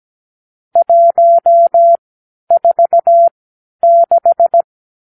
Morse code, heard as a keyed tone, about 17 wpm, spaced as for 10 wpm, 680 Hz. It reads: 146
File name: k7qo / Lesson 54 (9)